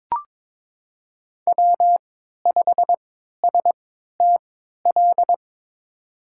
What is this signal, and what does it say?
Morse code, heard as a keyed tone, about 22 wpm, spaced as for 13 wpm, 700 Hz. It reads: W5STL
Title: USA Call Signs - 5 Characters - 3x Character Spacing - 22wpm